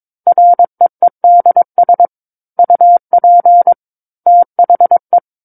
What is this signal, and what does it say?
Morse code, heard as a keyed tone, about 22 wpm, 700 Hz. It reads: REEDS UP THE
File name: Book One / 13-Chapter 13 - 22 WPM